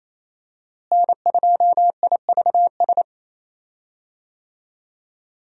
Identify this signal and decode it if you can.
Morse code, heard as a keyed tone, about 28 wpm, 700 Hz. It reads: N2IVS
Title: USA Call Signs - 5 Characters - Repeated 2x - 28wpm